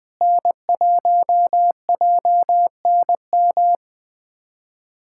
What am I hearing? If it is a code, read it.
Morse code, heard as a keyed tone, about 21 wpm, 700 Hz. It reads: N1JNM
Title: USA Callsign DE Callsign - 5 Characters - 20wpm